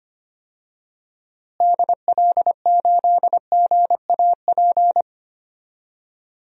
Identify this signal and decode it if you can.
Morse code, heard as a keyed tone, about 25 wpm, 700 Hz. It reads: DL8GAP
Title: DXCC-100 Call Signs - 6 Characters - Repeated 2x - 25wpm